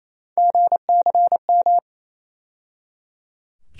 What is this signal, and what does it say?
Morse code, heard as a keyed tone, about 27 wpm, 700 Hz. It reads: GCM
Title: USA Call Sign Postfixes - 3 Letters 28wpm